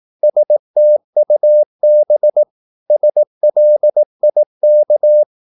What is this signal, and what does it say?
Morse code, heard as a keyed tone, about 18 wpm, 600 Hz. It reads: STUB SLIK
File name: Korte ord DK 18wpm